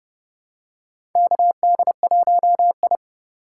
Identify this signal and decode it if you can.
Morse code, heard as a keyed tone, about 30 wpm, 700 Hz. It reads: KD1I
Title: USA Call Signs - 4 Characters - 4x Character Spacing - 30wpm